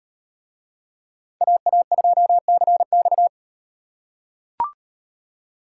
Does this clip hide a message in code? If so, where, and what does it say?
Yes: Morse code from 1.4 to 3.3 s, heard as a keyed tone, about 38 wpm, 700 Hz. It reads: AA2CX